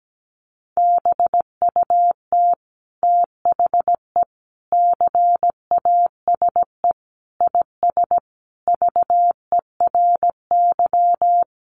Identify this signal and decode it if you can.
Morse code, heard as a keyed tone, about 17 wpm, 700 Hz. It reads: BUT THE CASE IS VERY
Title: Sentences from Top 700 Words 17wpm